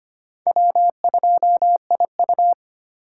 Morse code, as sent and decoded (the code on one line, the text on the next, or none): .-- ..--- .. ..-
W2IU